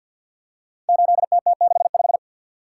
Morse code, heard as a keyed tone, about 50 wpm, 700 Hz. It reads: 8TT65
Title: Code Groups - 5 Letter:Numbers - 50wpm